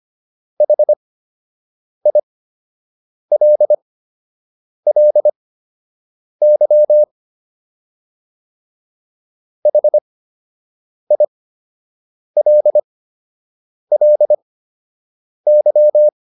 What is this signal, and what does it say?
Morse code, heard as a keyed tone, about 25 wpm, spaced as for 7 wpm, 600 Hz. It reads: HILLY HILLY